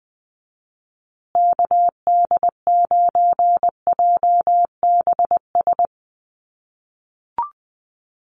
Morse code, heard as a keyed tone, about 20 wpm, 700 Hz. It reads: KD9JBS
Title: USA Call Signs Repeated 3x 20wpm